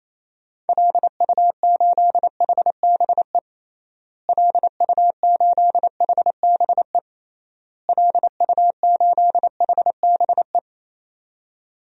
Morse code, heard as a keyed tone, about 27 wpm, 700 Hz. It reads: LU8HBE LU8HBE LU8HBE
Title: DXCC-100 Call Signs - 6 Characters - Repeated 3x - 28wpm